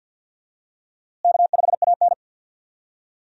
Morse code, heard as a keyed tone, about 47 wpm, 700 Hz. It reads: K5AN
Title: USA Call Signs - 4 Characters - Repeated 2x - 50wpm